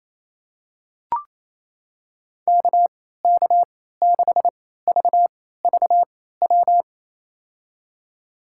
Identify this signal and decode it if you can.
Morse code, heard as a keyed tone, about 28 wpm, spaced as for 16 wpm, 700 Hz. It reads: KK6VVW
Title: USA Call Signs - 6 Characters - 3x Character Spacing - 28wpm